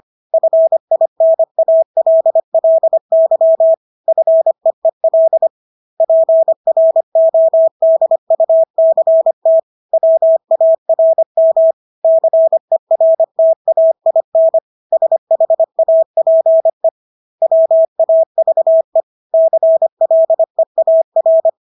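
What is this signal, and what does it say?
Morse code, heard as a keyed tone, about 25 wpm, 650 Hz. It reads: FINALLY FEEL PRODUCT WARM CERTAIN SHAPE WAVE CLEAR
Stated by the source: Morse practice of English words at 25 wpm from CW med Gnister.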